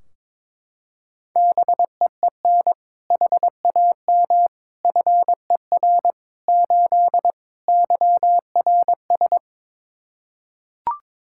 Morse code, heard as a keyed tone, about 22 wpm, 700 Hz. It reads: BEEN HAM FER 8 YRS
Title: Practice QSO - Exchange 3 - More Info - 22wpm